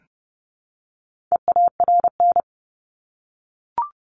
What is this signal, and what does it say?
Morse code, heard as a keyed tone, about 30 wpm, 700 Hz. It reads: EARN
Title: Top 1251-1500 Words 30wpm